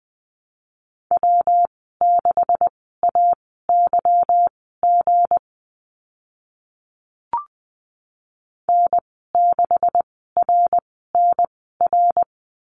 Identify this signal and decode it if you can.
Morse code, heard as a keyed tone, about 20 wpm, spaced as for 15 wpm, 700 Hz. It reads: W6AYG N6RNR
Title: 161 Lesson 26g - Letter g - Call Signs 20wpm